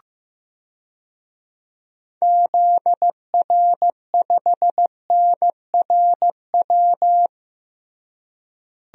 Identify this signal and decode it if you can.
Morse code, heard as a keyed tone, about 15 wpm, 700 Hz. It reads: ZR5NRW
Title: DXCC-100 Call Signs - 6 Characters - Repeated 3x - 15wpm